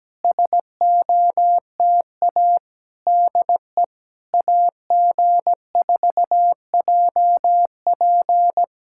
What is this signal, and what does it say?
Morse code, heard as a keyed tone, about 16 wpm, 700 Hz. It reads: SOTA DE AG4JP
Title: SOTA QSO Practice 17wpm